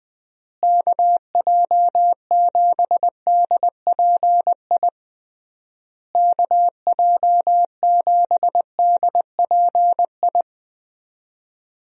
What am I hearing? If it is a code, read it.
Morse code, heard as a keyed tone, about 20 wpm, 700 Hz. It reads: KJ7DPI KJ7DPI
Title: USA Call Signs - 6 Characters - Repeated 3x - 20wpm